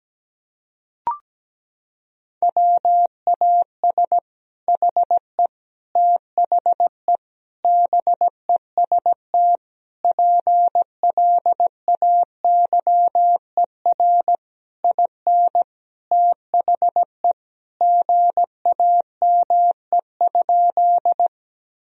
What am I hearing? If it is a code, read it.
Morse code, heard as a keyed tone, about 17 wpm, 700 Hz. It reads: WAS HE THE BEST PLAYER IN THE GAME?
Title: Sentences from Top 1000 Words - Review Part 4 - 17wpm